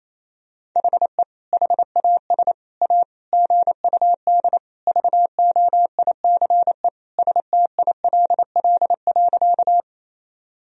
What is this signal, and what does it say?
Morse code, heard as a keyed tone, about 28 wpm, 700 Hz. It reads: HE HAS A GUD VOICE STILL.
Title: Sentences from Top 500 Words and Abbreviations 28wpm